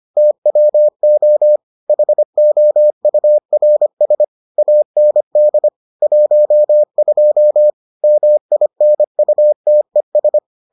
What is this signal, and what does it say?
Morse code, heard as a keyed tone, about 25 wpm, 600 Hz. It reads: TWO HOURS AND 12 MINUTES